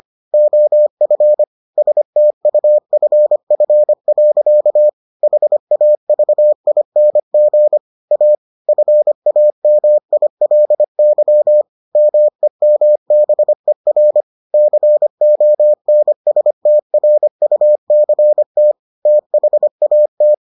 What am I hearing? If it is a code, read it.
Morse code, heard as a keyed tone, about 25 wpm, 600 Hz. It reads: OF STUFF. HAVING A FAMILY MEMBER CONSTRUCT THAT